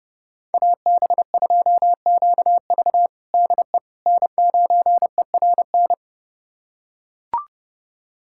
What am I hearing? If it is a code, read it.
Morse code, heard as a keyed tone, about 30 wpm, 700 Hz. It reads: AB2QV DE N9ERN
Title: USA Callsign DE Callsign - 5 Characters - 30wpm